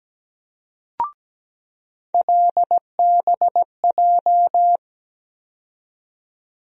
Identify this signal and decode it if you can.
Morse code, heard as a keyed tone, about 17 wpm, 700 Hz. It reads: LBJ
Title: USA Call Sign Postfixes - 3 Letters 17wpm